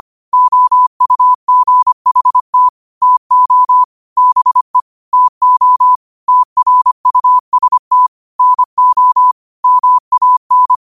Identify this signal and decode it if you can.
Morse code, heard as a keyed tone, about 25 wpm, 1000 Hz. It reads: OUGHT TO BE TO TRUST NO MAN
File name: qotdfiles / 25wpm